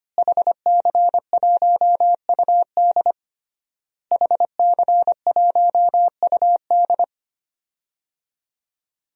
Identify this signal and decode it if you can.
Morse code, heard as a keyed tone, about 25 wpm, 700 Hz. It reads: HC1UD HC1UD
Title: DXCC-100 Call Signs - 5 Characters - Repeated 3x - 25wpm